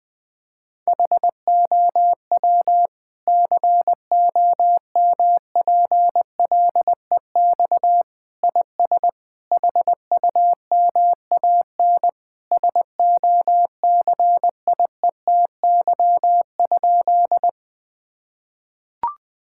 Morse code, heard as a keyed tone, about 21 wpm, 700 Hz. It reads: HOW COMPLEX IS HUMAN SOCIETY?
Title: Sentences from Top 1000 Words - Review Part 6 - 20wpm